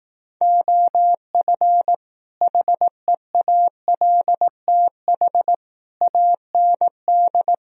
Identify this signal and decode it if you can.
Morse code, heard as a keyed tone, about 18 wpm, 700 Hz. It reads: OF HEALTH AND
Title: Speed-Racing - Sets of 5 Words 15wpm